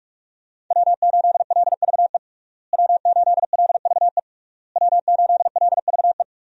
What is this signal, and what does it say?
Morse code, heard as a keyed tone, about 46 wpm, 700 Hz. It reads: W8LVE W8LVE W8LVE
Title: USA Call Signs - 5 Characters - Repeated 3x - 45wpm